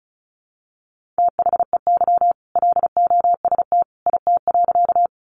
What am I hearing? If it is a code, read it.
Morse code, heard as a keyed tone, about 34 wpm, 700 Hz. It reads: THEY LOST IT.
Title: Sentences from Top 1400 Words 35wpm